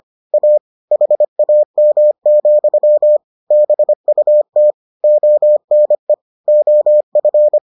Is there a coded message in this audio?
Yes: Morse code from 0.3 to 7.6 s, heard as a keyed tone, about 25 wpm, 600 Hz. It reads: A HAM, BUT ONE OF